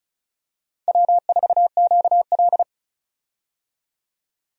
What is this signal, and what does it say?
Morse code, heard as a keyed tone, about 34 wpm, 700 Hz. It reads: W4QL
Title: USA Call Signs - 4 Characters - 35wpm